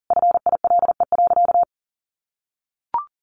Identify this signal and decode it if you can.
Morse code, heard as a keyed tone, about 40 wpm, 700 Hz. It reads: FILE.
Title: Sentences from Top 1200 Words 40wpm